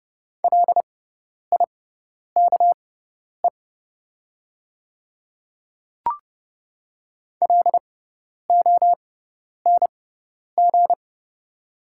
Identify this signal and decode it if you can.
Morse code, heard as a keyed tone, about 30 wpm, spaced as for 10 wpm, 700 Hz. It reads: LIKE LONG
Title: 6x Character Spacing - Top 200 Words - 30wpm